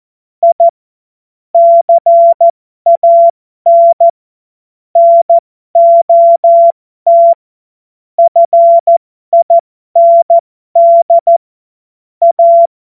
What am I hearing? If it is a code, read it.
Morse code, heard as a keyed tone, about 14 wpm, spaced as for 12 wpm, 680 Hz. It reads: I CAN NOT FIND A
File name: Morsekurs / Witze